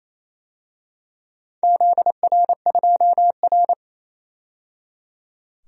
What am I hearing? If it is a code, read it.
Morse code, heard as a keyed tone, about 28 wpm, 700 Hz. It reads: ZR2R
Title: DXCC-100 Call Signs - 4 Characters - 28wpm